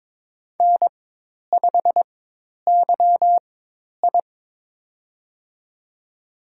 Morse code, heard as a keyed tone, about 21 wpm, spaced as for 10 wpm, 700 Hz. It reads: N5YI
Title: USA Call Signs - 4x Character Spacing - 22wpm